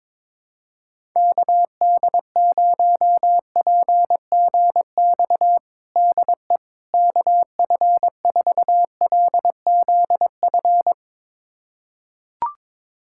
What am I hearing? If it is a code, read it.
Morse code, heard as a keyed tone, about 22 wpm, 700 Hz. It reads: KD0PGX DE KF4LZF
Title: USA Callsign DE Callsign 22wpm